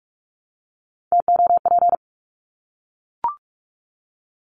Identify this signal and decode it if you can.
Morse code, heard as a keyed tone, about 44 wpm, 700 Hz. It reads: TOP
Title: Top 500 Words Repeated 45wpm